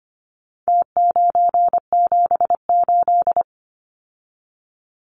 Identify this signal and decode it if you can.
Morse code, heard as a keyed tone, about 25 wpm, 700 Hz. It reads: T978
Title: Numbers - Contest Style - 4-Digit 25wpm